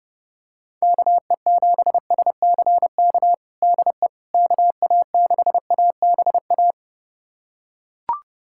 Morse code, heard as a keyed tone, about 32 wpm, 700 Hz. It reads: KE7SCK DE KA6ABA
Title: USA Callsign DE Callsign - 6 Characters - 30wpm